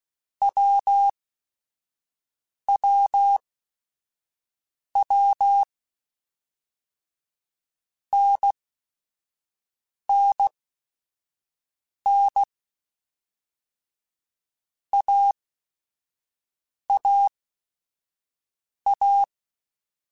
Morse code, heard as a keyed tone, about 16 wpm, spaced as for 5 wpm, 800 Hz. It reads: WWW NNN AAA
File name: Basic Sets / Basic Letter Groups Collective7 / AWUNDG@5